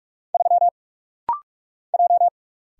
Morse code, heard as a keyed tone, about 44 wpm, 700 Hz. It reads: 3 J
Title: Single Letter-Number - Rapid-Fire 45wpm